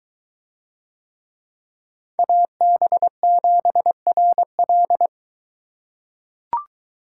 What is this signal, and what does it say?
Morse code, heard as a keyed tone, about 23 wpm, 700 Hz. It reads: AB7RL